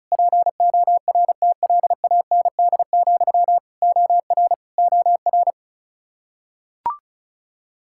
Morse code, heard as a keyed tone, about 36 wpm, 700 Hz. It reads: PORTLAND, OR OR
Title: Practice QSO - Exchange 1 - Beginning - 35wpm